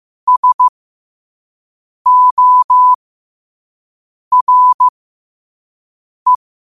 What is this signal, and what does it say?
Morse code, heard as a keyed tone, about 15 wpm, spaced as for 5 wpm, 1000 Hz. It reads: SORE
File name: qotdfiles / 5wpm